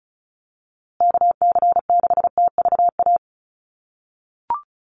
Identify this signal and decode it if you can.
Morse code, heard as a keyed tone, about 34 wpm, 700 Hz. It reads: KC6TVA
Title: USA Call Signs Repeated 3x 35wpm